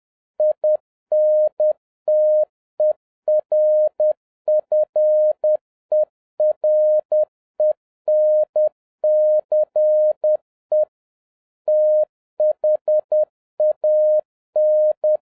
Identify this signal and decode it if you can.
Morse code, heard as a keyed tone, about 10 wpm, 610 Hz. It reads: INTERFERENCE THAN
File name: 2011-07-mp3 / n41 EXCITING QRP CW OPERATION 10wpm